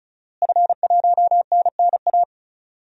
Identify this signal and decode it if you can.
Morse code, heard as a keyed tone, about 34 wpm, 700 Hz. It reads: F1NNA
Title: DXCC-100 Call Signs - 5 Characters - Repeated 3x - 35wpm